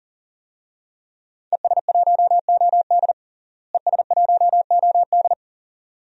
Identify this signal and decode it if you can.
Morse code, heard as a keyed tone, about 40 wpm, 700 Hz. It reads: ES1OD ES1OD
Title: DXCC-100 Call Signs - 5 Characters - Repeated 2x - 40wpm